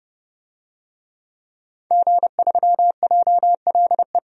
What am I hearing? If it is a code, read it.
Morse code, heard as a keyed tone, about 30 wpm, 700 Hz. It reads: G3JLE
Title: DXCC-100 Call Signs - 5 Characters - 30wpm